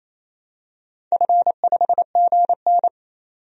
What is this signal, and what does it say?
Morse code, heard as a keyed tone, about 27 wpm, 700 Hz. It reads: F5GN
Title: DXCC-100 Call Signs - 4 Characters - 28wpm